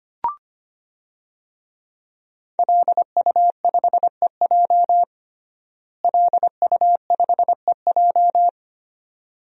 Morse code, heard as a keyed tone, about 25 wpm, 700 Hz. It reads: LU5EJ LU5EJ
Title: DXCC-100 Call Signs - 5 Characters - Repeated 2x - 25wpm